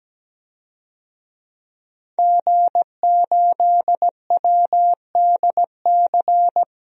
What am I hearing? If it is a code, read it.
Morse code, heard as a keyed tone, about 17 wpm, 700 Hz. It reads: G8WDC
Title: DXCC-100 Call Signs - 5 Characters - 17wpm